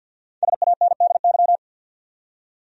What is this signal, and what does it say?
Morse code, heard as a keyed tone, about 50 wpm, 700 Hz. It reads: SANDY
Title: USA Common Names - Max 5 Letters - 50wpm